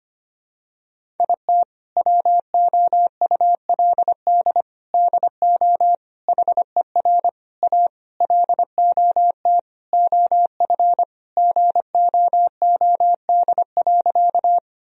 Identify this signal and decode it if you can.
Morse code, heard as a keyed tone, about 25 wpm, 700 Hz. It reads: IT WOULD DO HER A LOT OF GOOD.
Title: Sentences from Top 1000 Words - Review Part 3 - 25wpm